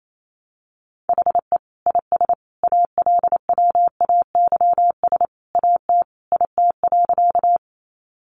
Morse code, heard as a keyed tone, about 28 wpm, 700 Hz. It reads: HE IS ALWAYS AT IT.
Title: Sentences from Top 300 Words 28wpm